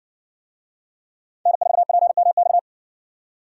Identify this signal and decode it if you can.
Morse code, heard as a keyed tone, about 60 wpm, 700 Hz. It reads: N4PMX